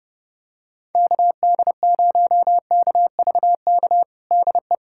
Morse code, heard as a keyed tone, about 30 wpm, 700 Hz. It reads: KD0KVK DE
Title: USA Callsign DE Callsign - 6 Characters - 30wpm